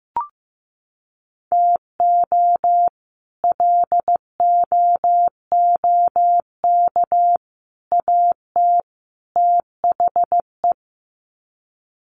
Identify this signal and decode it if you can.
Morse code, heard as a keyed tone, about 15 wpm, 700 Hz. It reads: TO LOOK AT THE
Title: Sets of 4 Words 15wpm